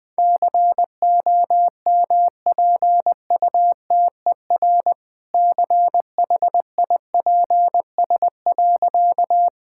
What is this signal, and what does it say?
Morse code, heard as a keyed tone, about 20 wpm, 700 Hz. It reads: COMPUTER CHIPS.
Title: All-Age Appropriate Jokes - 20wpm